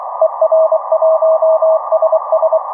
Morse code, heard as a keyed tone, about 23 wpm, 640 Hz, with noise. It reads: ER1SS
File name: er1ss 4m es -1